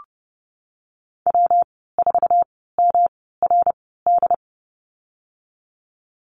Morse code, heard as a keyed tone, about 30 wpm, spaced as for 17 wpm, 700 Hz. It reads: W4MRD
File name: Lesson 19g - Letter w - Call Signs 30wpm